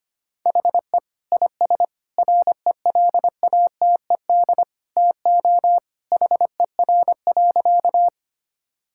Morse code, heard as a keyed tone, about 25 wpm, 700 Hz. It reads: HE IS RELATED TO HER.
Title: Sentences from Top 1000 Words - Review Part 5 - 25wpm